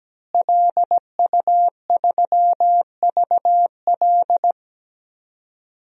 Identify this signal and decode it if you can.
Morse code, heard as a keyed tone, about 17 wpm, 700 Hz. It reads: LU3VL
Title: DXCC-100 Call Signs - 5 Characters - Repeated 2x - 17wpm